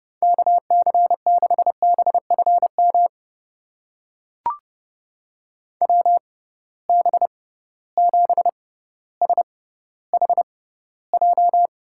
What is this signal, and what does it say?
Morse code, heard as a keyed tone, about 29 wpm, spaced as for 15 wpm, 700 Hz. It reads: KC6BFM WB7SHJ
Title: USA Call Signs - 6 Characters - 6x Character Spacing - 30wpm